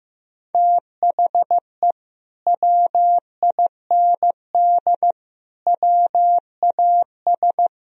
Morse code, heard as a keyed tone, about 14 wpm, 700 Hz. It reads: THE WIND WAS
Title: Sentences from Top 1000 Words - Review Part 2 - 15wpm